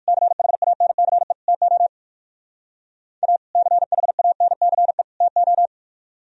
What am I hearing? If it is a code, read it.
Morse code, heard as a keyed tone, about 57 wpm, 700 Hz. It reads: CHANCE TO A CHANCE TO